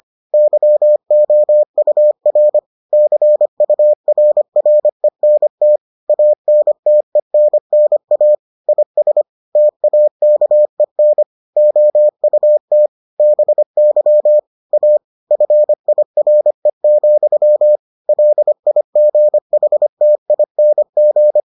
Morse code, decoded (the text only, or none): YOUR CURRENT ANTENNA IS TAKEN OUT BY A FIRE, LIGHTING